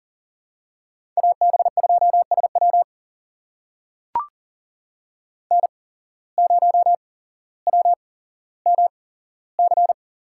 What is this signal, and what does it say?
Morse code, heard as a keyed tone, about 40 wpm, spaced as for 17 wpm, 700 Hz. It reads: AB2SW N0WMC
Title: USA Call Signs - 5 Characters - 8x Character Spacing - 40wpm